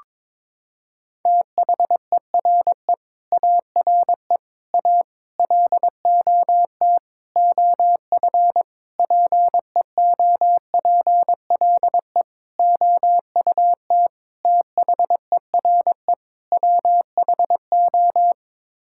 Morse code, heard as a keyed tone, about 22 wpm, 700 Hz. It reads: THERE ARE A LOT OF PEOPLE OUT THERE WHO
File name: Sets of 9 Words - 22wpm